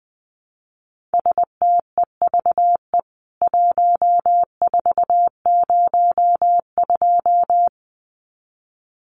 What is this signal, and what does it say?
Morse code, heard as a keyed tone, about 20 wpm, 700 Hz. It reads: STEVE 1402
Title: CWT Exchange 20wpm